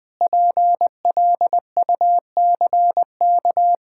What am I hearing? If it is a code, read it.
Morse code, heard as a keyed tone, about 21 wpm, 700 Hz. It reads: PLUCK
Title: Pangram Challenge v2 - 20wpm